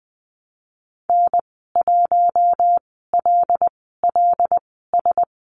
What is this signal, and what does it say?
Morse code, heard as a keyed tone, about 19 wpm, spaced as for 14 wpm, 700 Hz. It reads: N1LLS
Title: 154 Lesson 25g - Letter p - Call Signs 20wpm